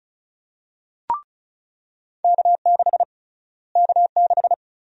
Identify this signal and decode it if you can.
Morse code, heard as a keyed tone, about 36 wpm, 700 Hz. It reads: K6 K6
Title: USA Call Sign Prefixes - Repeated 2x - 35wpm